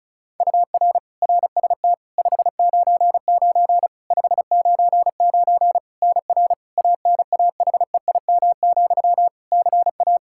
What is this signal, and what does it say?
Morse code, heard as a keyed tone, about 36 wpm, 700 Hz. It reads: UR RST 599 599 NR ANAHEIM, CA
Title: Practice QSO - Exchange 1 - Beginning - 35wpm